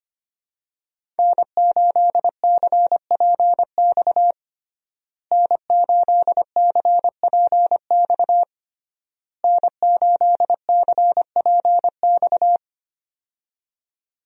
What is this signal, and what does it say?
Morse code, heard as a keyed tone, about 25 wpm, 700 Hz. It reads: N8CPX N8CPX N8CPX
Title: USA Call Signs - 5 Characters - Repeated 3x - 25wpm